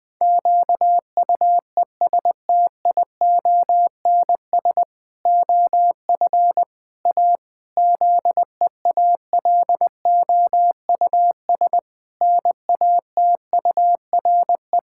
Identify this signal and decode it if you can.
Morse code, heard as a keyed tone, about 19 wpm, 700 Hz. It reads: QUESTIONS OF A ZEALOUS NATURE
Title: Pangram Challenge v2 - 20wpm